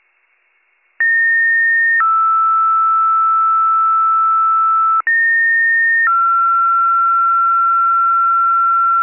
Two-tone page, heard as a one-second tone, then a three-second tone, about 1850 then 1320 Hz, about 1850 then 1360 Hz. It is heard from 1.0 to 9.0 s.